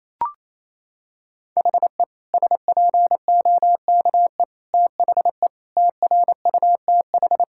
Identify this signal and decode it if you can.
Morse code, heard as a keyed tone, about 28 wpm, 700 Hz. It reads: HE SPOKE THE TRUTH
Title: Sentences from Top 1600 Words 28wpm